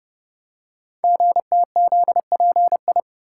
Morse code, heard as a keyed tone, about 30 wpm, 700 Hz. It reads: GTZPI
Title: Code Groups - 5 Letters - 30wpm